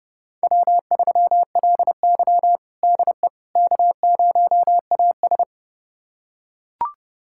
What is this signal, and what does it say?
Morse code, heard as a keyed tone, about 30 wpm, 700 Hz. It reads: W3LY DE K0AS
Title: USA Callsign DE Callsign - 4 Characters - 30wpm